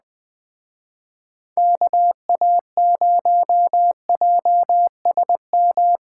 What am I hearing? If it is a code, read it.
Morse code, heard as a keyed tone, about 20 wpm, 700 Hz. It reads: KA0JSM